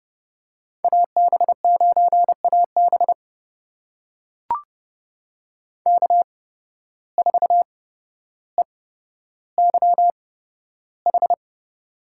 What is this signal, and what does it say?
Morse code, heard as a keyed tone, about 31 wpm, spaced as for 13 wpm, 700 Hz. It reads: AB9AB K4EYH
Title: USA Call Signs - 5 Characters - 8x Character Spacing - 30wpm